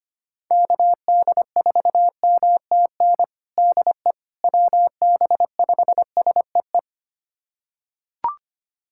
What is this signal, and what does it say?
Morse code, heard as a keyed tone, about 25 wpm, 700 Hz. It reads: KD4MTN DE WB5SEE
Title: USA Callsign DE Callsign - 6 Characters - 25wpm